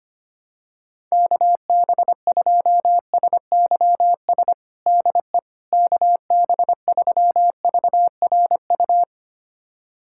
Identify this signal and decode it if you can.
Morse code, heard as a keyed tone, about 25 wpm, 700 Hz. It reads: KB2SYS DE KB3VRU
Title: USA Callsign DE Callsign - 6 Characters - 25wpm